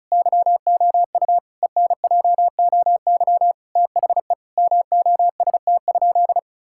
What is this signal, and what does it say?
Morse code, heard as a keyed tone, about 34 wpm, 700 Hz. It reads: YOU ENJOY THE MOST?
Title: Sentences from Top 1000 Words - Review Part 5 - 35wpm